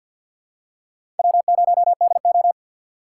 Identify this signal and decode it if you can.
Morse code, heard as a keyed tone, about 51 wpm, 700 Hz. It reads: W0DO
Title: USA Call Signs - 4 Characters - 2x Character Spacing - 50wpm